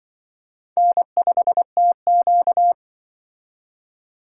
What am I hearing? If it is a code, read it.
Morse code, heard as a keyed tone, about 24 wpm, 700 Hz. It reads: N5TQ
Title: Speed Racing - USA 1x2 Call Signs 20wpm